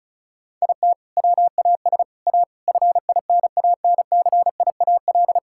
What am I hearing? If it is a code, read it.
Morse code, heard as a keyed tone, about 34 wpm, 700 Hz. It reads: IT WAS A FINANCIAL
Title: Sentences from Top 1000 Words - Review Part 6 - 35wpm